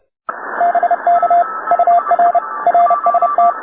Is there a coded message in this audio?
Yes: Morse code from 0.6 to 3.5 s, heard as a keyed tone, about 30 wpm, 700 Hz, with radio noise. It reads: BK UR RST